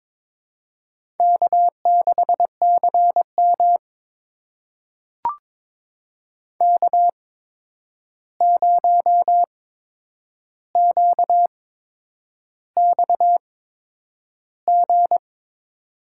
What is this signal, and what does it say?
Morse code, heard as a keyed tone, about 22 wpm, spaced as for 9 wpm, 700 Hz. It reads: K6CM K0QXG
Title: USA Call Signs - 8x Character Spacing - 22wpm